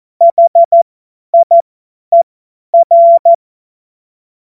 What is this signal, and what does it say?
Morse code, heard as a keyed tone, about 14 wpm, spaced as for 10 wpm, 680 Hz. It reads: HIER
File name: Morsekurs / Witze / Witz 25. Mai 2021, 50 F70